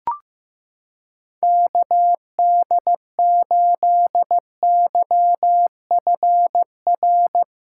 Morse code, heard as a keyed tone, about 15 wpm, 700 Hz. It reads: KD8YFR
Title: USA Callsign DE Callsign 15wpm